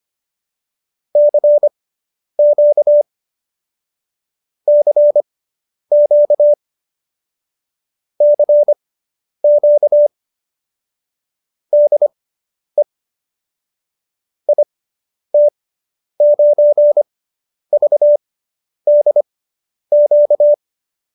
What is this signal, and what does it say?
Morse code, heard as a keyed tone, about 25 wpm, spaced as for 10 wpm, 590 Hz. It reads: CQ CQ CQ DE IT9VDQ